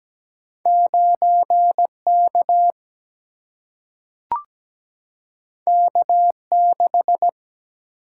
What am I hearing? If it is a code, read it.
Morse code, heard as a keyed tone, about 17 wpm, 700 Hz. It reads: 9K K6
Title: Single-Letter Single-Number 17wpm